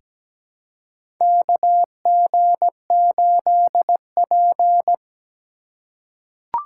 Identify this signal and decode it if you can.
Morse code, heard as a keyed tone, about 17 wpm, 700 Hz. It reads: KG8P
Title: USA Call Signs - 3x Character Spacing - 17wpm